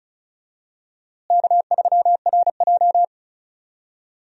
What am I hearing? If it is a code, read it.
Morse code, heard as a keyed tone, about 34 wpm, 700 Hz. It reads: K3RJ
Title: USA Call Signs - 4 Characters - Repeated 3x - 35wpm